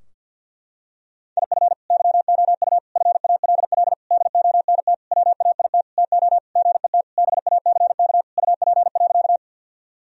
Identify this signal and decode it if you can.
Morse code, heard as a keyed tone, about 52 wpm, 700 Hz. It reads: IF YOU FALL DONT WAIT TO GET BACK UP.
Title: Sentences from Top 1000 Words - Review Part 2 - 50wpm